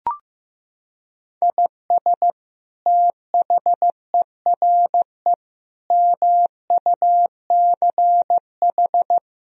Morse code, heard as a keyed tone, about 15 wpm, 700 Hz. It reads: IS THERE MUCH
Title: Sentences from Top 1000 Words - Review Part 4 - 15wpm